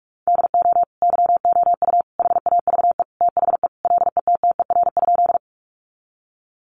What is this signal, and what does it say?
Morse code, heard as a keyed tone, about 44 wpm, 700 Hz. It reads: DO YOU HAVE THE LETTER?
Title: Sentences from Top 300 Words 45wpm